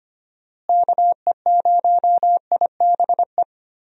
Morse code, heard as a keyed tone, about 25 wpm, 700 Hz. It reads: KE0IBE
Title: USA Call Signs - 6 Characters - Repeated 3x - 25wpm